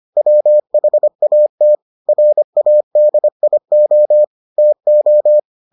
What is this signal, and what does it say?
Morse code, heard as a keyed tone, about 25 wpm, 600 Hz. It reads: WHAT RADIO TO